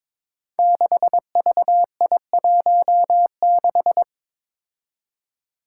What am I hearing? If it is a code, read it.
Morse code, heard as a keyed tone, about 22 wpm, 700 Hz. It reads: 6VI16
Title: Five Challenging Letter-Number Combinations 22wpm